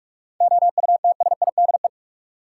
Morse code, heard as a keyed tone, about 42 wpm, 700 Hz. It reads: OUTSIDE
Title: Sentences from Top 900 Words - New Word Drill - 45wpm